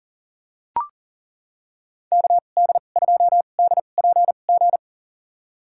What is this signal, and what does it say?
Morse code, heard as a keyed tone, about 40 wpm, spaced as for 29 wpm, 700 Hz. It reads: KD2DPG
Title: USA Call Signs - 6 Characters - 2x Character Spacing - 40wpm